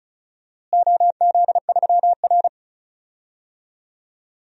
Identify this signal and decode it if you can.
Morse code, heard as a keyed tone, about 36 wpm, 700 Hz. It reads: OZ3R
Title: DXCC-100 Call Signs - 4 Characters - Repeated 3x - 35wpm